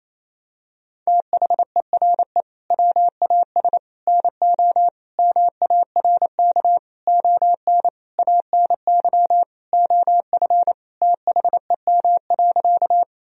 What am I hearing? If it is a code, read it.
Morse code, heard as a keyed tone, about 28 wpm, 700 Hz. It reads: THERE WAS NO MARK ON ANY OF THEM.
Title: Sentences from Top 1000 Words - Review Part 1 - 28wpm